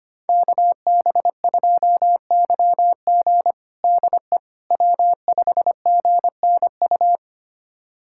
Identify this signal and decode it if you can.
Morse code, heard as a keyed tone, about 25 wpm, 700 Hz. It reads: KB2YG DE W5GNU
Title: USA Callsign DE Callsign - 5 Characters - 25wpm